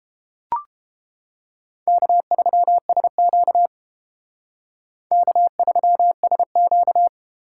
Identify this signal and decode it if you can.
Morse code, heard as a keyed tone, about 29 wpm, 700 Hz. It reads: K3SQ K3SQ